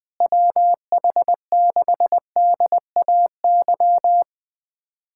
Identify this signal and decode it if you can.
Morse code, heard as a keyed tone, about 20 wpm, 700 Hz. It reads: WH6DAY
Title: USA Callsign DE Callsign - 6 Characters - 20wpm